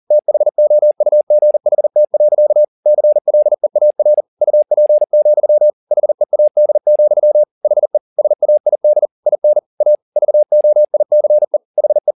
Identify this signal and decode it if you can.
Morse code, heard as a keyed tone, about 40 wpm, 600 Hz. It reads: THOUGHT. CLEAR UP, HEAD, HE SAID IN A VOICE HE
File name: the old man and the sea 2 40wpm